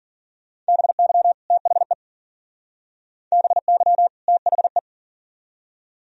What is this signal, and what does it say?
Morse code, heard as a keyed tone, about 49 wpm, 700 Hz. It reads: BY THE BY THE
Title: Speed-Racing - Sets of 2 Words 40wpm